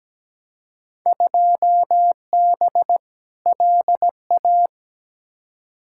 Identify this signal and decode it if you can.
Morse code, heard as a keyed tone, about 17 wpm, 700 Hz. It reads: 2B LA
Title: Field Day Exchanges Encore - 17wpm